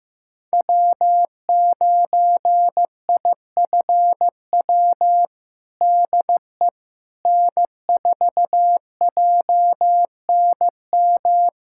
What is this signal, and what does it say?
Morse code, heard as a keyed tone, about 15 wpm, 700 Hz. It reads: W9IFW DE N4JNM
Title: USA Callsign DE Callsign - 5 Characters - 15wpm